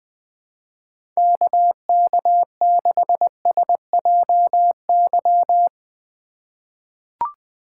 Morse code, heard as a keyed tone, about 21 wpm, 700 Hz. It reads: KK6SJY
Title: USA Call Signs - 3x Character Spacing - 20wpm